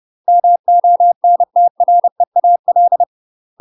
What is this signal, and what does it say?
Morse code, heard as a keyed tone, about 30 wpm, 700 Hz. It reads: MONTREAL
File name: UK MASTER QSOs TRAINING / 30WPM / UK MASTER QSOs 0048